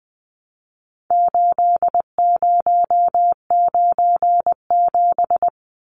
Morse code, heard as a keyed tone, about 20 wpm, 700 Hz. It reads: 8097
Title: Numbers - 4-Digit 20wpm